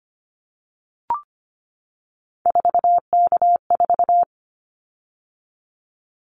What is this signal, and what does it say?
Morse code, heard as a keyed tone, about 25 wpm, 700 Hz. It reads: 4K4
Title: Callsign Prefixes for Top 100 DXCC - 25wpm